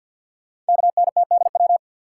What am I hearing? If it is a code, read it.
Morse code, heard as a keyed tone, about 50 wpm, 690 Hz. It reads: KNTDW
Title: Code Groups - 5 Letters - 50wpm